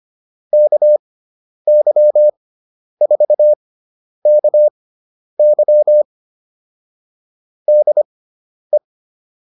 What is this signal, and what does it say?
Morse code, heard as a keyed tone, about 25 wpm, spaced as for 10 wpm, 600 Hz. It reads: KY4KY DE
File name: QSO108 10